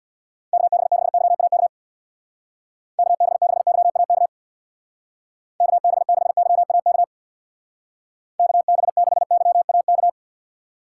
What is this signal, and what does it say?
Morse code, heard as a keyed tone, about 61 wpm, 700 Hz. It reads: KB6YAK KB6YAK KB6YAK KB6YAK
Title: Speed Racing - USA Call Signs 50wpm